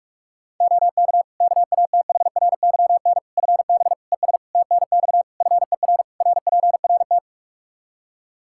Morse code, heard as a keyed tone, about 46 wpm, 700 Hz. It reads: OK KATHRYN FB ES TNX FER RPRT
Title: K1USN SST QSO Practice 45wpm